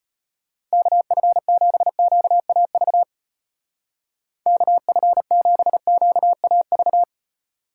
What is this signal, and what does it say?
Morse code, heard as a keyed tone, about 32 wpm, 700 Hz. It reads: KF7QAV KF7QAV